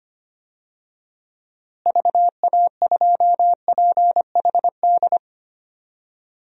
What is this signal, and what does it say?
Morse code, heard as a keyed tone, about 25 wpm, 700 Hz. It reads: VA2PHD
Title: DXCC-100 Call Signs - 6 Characters - Repeated 2x - 25wpm